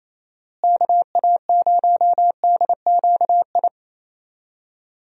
Morse code, heard as a keyed tone, about 28 wpm, 700 Hz. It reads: KA0DQI
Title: USA Call Signs - 6 Characters - Repeated 2x - 28wpm